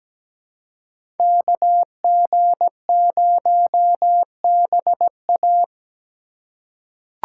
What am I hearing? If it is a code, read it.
Morse code, heard as a keyed tone, about 17 wpm, 700 Hz. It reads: KG0BA
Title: USA Call Signs - 3x Character Spacing - 17wpm